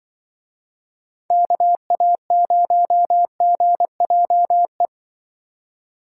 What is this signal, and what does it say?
Morse code, heard as a keyed tone, about 23 wpm, 700 Hz. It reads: KA0GJE
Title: Speed Racing - USA Call Signs 20wpm